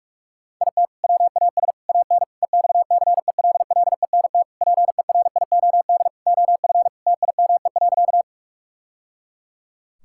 Morse code, heard as a keyed tone, about 46 wpm, 700 Hz. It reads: IT WAS AN EXCELLENT PERIOD OF TIME.
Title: Sentences from Top 1000 Words - Review Part 5 - 45wpm